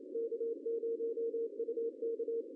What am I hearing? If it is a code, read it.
Morse code, heard as a keyed tone, about 29 wpm, 460 Hz. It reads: K0UK
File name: CW / cqww-cw-k5tr